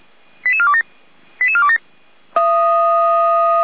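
Five-tone selective call (ZVEI), heard as quick runs of five tones, tones of about 70 ms. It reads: 88427 88427
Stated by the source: ZVEI five-tone call received by SDR.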